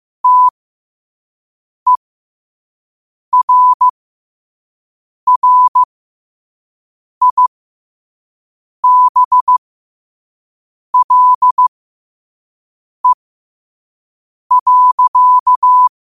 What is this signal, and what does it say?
Morse code, heard as a keyed tone, about 15 wpm, spaced as for 5 wpm, 1000 Hz. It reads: TERRIBLE.
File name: qotdfiles / 5wpm